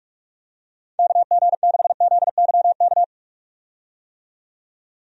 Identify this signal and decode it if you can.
Morse code, heard as a keyed tone, about 46 wpm, 700 Hz. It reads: KG6ZYK
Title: USA Call Signs - 6 Characters - 45wpm